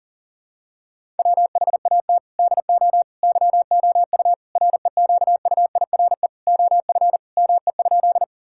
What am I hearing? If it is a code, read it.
Morse code, heard as a keyed tone, about 37 wpm, 700 Hz. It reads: WHAT DO YOU REQUIRE OF ME?
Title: Sentences from Top 1000 Words - Review Part 6 - 40wpm